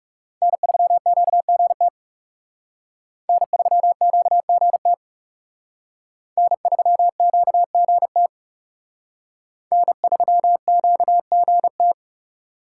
Morse code, heard as a keyed tone, about 38 wpm, 700 Hz. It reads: N3QGT N3QGT N3QGT N3QGT